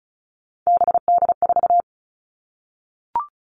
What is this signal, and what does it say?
Morse code, heard as a keyed tone, about 34 wpm, 700 Hz. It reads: BD4
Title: Callsign Prefixes for Top 100 DXCC - 35wpm